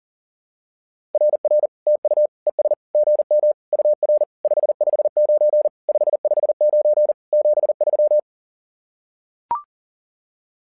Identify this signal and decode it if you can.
Morse code, heard as a keyed tone, about 40 wpm, 600 Hz. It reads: RR TU ES GM UR 559 559 73
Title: SOTA QSO Practice 40wpm